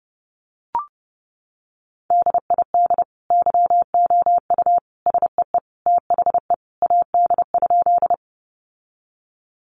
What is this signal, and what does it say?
Morse code, heard as a keyed tone, about 30 wpm, 700 Hz. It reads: DID YOU SEE THE AD?
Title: Sentences from Top 1300 Words 30wpm